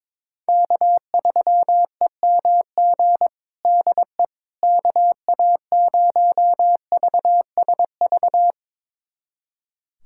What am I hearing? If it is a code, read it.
Morse code, heard as a keyed tone, about 22 wpm, 700 Hz. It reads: K3EMG DE KA0VSV
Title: USA Callsign DE Callsign 22wpm